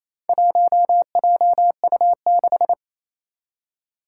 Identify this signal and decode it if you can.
Morse code, heard as a keyed tone, about 28 wpm, 700 Hz. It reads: 1JU6
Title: Four Challenging Letter-Number Combinations 28wpm